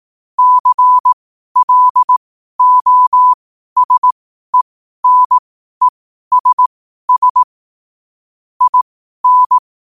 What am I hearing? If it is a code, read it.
Morse code, heard as a keyed tone, about 18 wpm, spaced as for 12 wpm, 1000 Hz. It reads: CLOSENESS IN
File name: qotdfiles / 12wpm